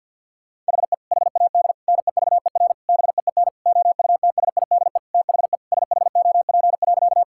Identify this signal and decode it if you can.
Morse code, heard as a keyed tone, about 50 wpm, 700 Hz. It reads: HE HAD NEVER BEEN OUTSIDE THE SHOP.
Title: Sentences from Top 1600 Words 50wpm